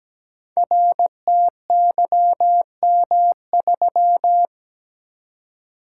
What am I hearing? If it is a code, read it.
Morse code, heard as a keyed tone, about 17 wpm, 700 Hz. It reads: RTYM3
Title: Code Groups - 5 Letter:Numbers - 17wpm